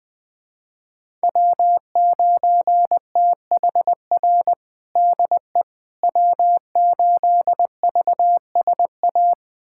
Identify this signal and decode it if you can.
Morse code, heard as a keyed tone, about 20 wpm, 700 Hz. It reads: W9THR DE W8VSA
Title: USA Callsign DE Callsign - 5 Characters - 20wpm